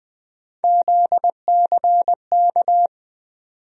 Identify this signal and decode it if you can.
Morse code, heard as a keyed tone, about 20 wpm, 700 Hz. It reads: ZCK
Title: USA Call Sign Postfixes - 3 Letters 20wpm